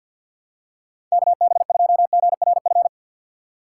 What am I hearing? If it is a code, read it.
Morse code, heard as a keyed tone, about 49 wpm, 700 Hz. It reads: KB2GRF
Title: USA Call Signs - 6 Characters - 4x Character Spacing - 50wpm